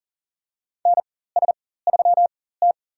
Morse code, heard as a keyed tone, about 40 wpm, spaced as for 19 wpm, 700 Hz. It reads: NS3T
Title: USA Call Signs - 4 Characters - 4x Character Spacing - 40wpm